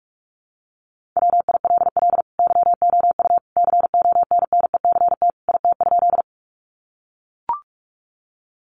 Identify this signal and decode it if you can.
Morse code, heard as a keyed tone, about 44 wpm, 700 Hz. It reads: WILL YOU CONNECT IT?
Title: Sentences from Top 1500 Words 45wpm